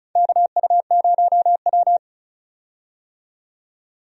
Morse code, heard as a keyed tone, about 34 wpm, 700 Hz. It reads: KU0W
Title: USA Call Signs - 4 Characters - 35wpm